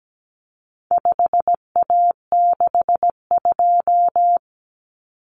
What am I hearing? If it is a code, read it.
Morse code, heard as a keyed tone, about 17 wpm, 700 Hz. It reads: 5A62
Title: Common Cut Numbers - 4-Digit 17wpm